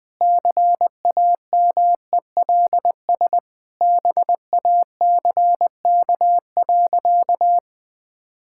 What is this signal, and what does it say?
Morse code, heard as a keyed tone, about 20 wpm, 700 Hz. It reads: CAMELS BACK.
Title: Sentences with 8-Word Phrases - 20wpm